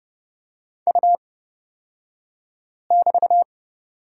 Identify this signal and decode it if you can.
Morse code, heard as a keyed tone, about 30 wpm, spaced as for 5 wpm, 700 Hz. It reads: U<BT>
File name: Lesson 37d - Letter x - Two Characters 30wpm